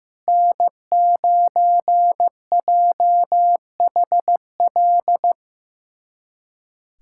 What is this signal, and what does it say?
Morse code, heard as a keyed tone, about 15 wpm, 700 Hz. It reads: N9JHL
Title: USA Callsign DE Callsign - 5 Characters - 15wpm